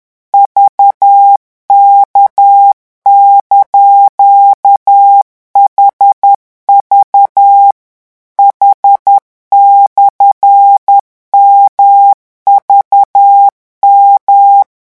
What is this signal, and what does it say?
Morse code, heard as a keyed tone, about 10 wpm, 800 Hz. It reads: VK)HV H/MVM
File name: morse mp3 / revision16-vitesse 10 mots